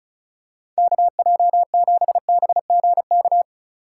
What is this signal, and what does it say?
Morse code, heard as a keyed tone, about 36 wpm, 700 Hz. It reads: KJ7BGK
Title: USA Call Signs - 6 Characters - 3x Character Spacing - 35wpm